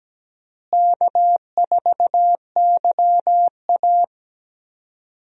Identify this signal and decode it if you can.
Morse code, heard as a keyed tone, about 17 wpm, 700 Hz. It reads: K4YA
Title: USA Call Signs - 4 Characters - Repeated 2x - 17wpm